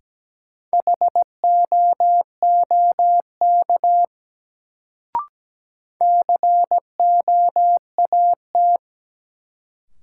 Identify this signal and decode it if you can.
Morse code, heard as a keyed tone, about 17 wpm, 700 Hz. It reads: HOOK COAT
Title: Sentences from Top 1600 Words - New Word Drill Alt - 17wpm